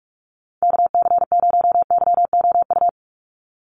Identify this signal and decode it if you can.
Morse code, heard as a keyed tone, about 44 wpm, 700 Hz. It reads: KC0YOU
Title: USA Call Signs 45wpm